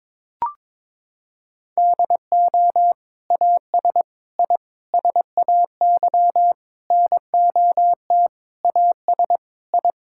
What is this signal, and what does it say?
Morse code, heard as a keyed tone, about 22 wpm, 700 Hz. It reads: DO AS I SAY NOT AS I
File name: practice / 22wpm / Sentences with 8-Word Phrases - 22wpm